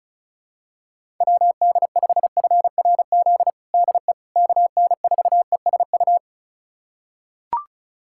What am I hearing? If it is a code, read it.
Morse code, heard as a keyed tone, about 36 wpm, 700 Hz. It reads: WD5FRZ DE KN4ESU
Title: USA Callsign DE Callsign 35wpm